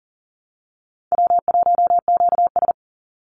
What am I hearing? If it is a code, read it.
Morse code, heard as a keyed tone, about 40 wpm, 700 Hz. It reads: W1QS
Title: Easy Call Signs Repeated 2x - 40wpm